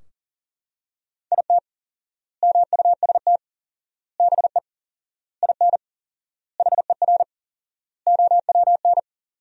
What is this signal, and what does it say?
Morse code, heard as a keyed tone, about 40 wpm, 700 Hz. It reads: IT MUST BE IN HER OWN
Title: 4x Word-Spacing - Sentences from Top 300 Words - 40wpm